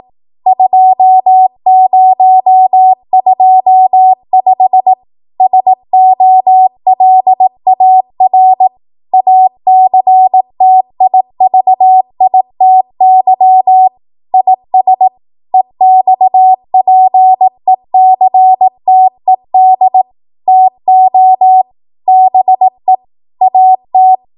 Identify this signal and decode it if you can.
Morse code, heard as a keyed tone, about 18 wpm, 750 Hz. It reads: 2025 SOLAR ACTIVITY IS EXPECTED TO BE AT
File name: Morse / 18WPM